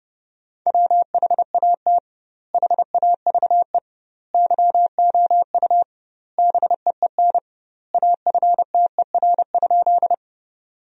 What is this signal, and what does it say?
Morse code, heard as a keyed tone, about 30 wpm, 700 Hz. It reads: WHAT HAVE YOU BEEN AFTER?
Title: 2x Word-Spacing - Sentences from Top 200 Words - 30wpm